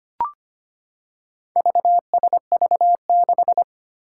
Morse code, heard as a keyed tone, about 25 wpm, 700 Hz. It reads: VSV6
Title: Four Challenging Letter-Number Combinations 25wpm